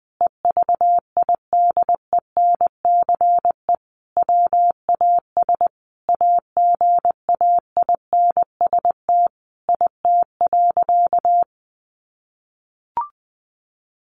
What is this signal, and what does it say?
Morse code, heard as a keyed tone, about 20 wpm, 700 Hz. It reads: EVIDENCE WAS AGAINST IT.
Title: Sentences from Top 1400 Words 20wpm